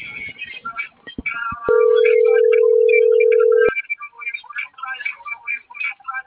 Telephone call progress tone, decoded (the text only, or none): ringback tone